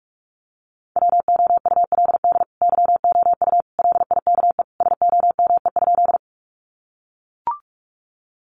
Morse code, heard as a keyed tone, about 44 wpm, 700 Hz. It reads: WOULD YOU LIKE SOME?
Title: Sentences from Top 100 Words 45wpm